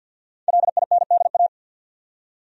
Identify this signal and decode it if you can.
Morse code, heard as a keyed tone, about 52 wpm, 700 Hz. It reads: LINDA
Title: USA Common Names - Max 5 Letters - 50wpm